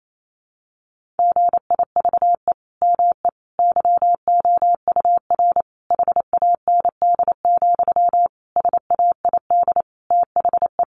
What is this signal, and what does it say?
Morse code, heard as a keyed tone, about 28 wpm, 700 Hz. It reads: GIVE ME YOUR HAND, SAID THE
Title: Sentences from Top 1500 Words 28wpm